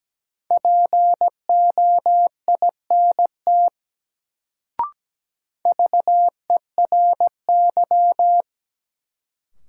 Morse code, heard as a keyed tone, about 17 wpm, 700 Hz. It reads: POINT VERY
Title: Sentences from Top 200 Words - New Word Drill - 17wpm